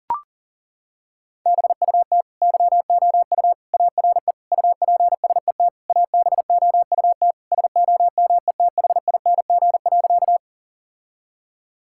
Morse code, heard as a keyed tone, about 40 wpm, 700 Hz. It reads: BUT YOU ARE UPSET ABOUT SOMETHING.
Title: Sentences from Top 1000 Words - Review Part 5 - 40wpm